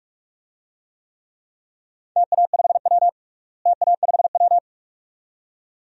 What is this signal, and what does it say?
Morse code, heard as a keyed tone, about 46 wpm, 700 Hz. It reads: TA5W TA5W
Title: DXCC-100 Call Signs - 4 Characters - Repeated 2x - 45wpm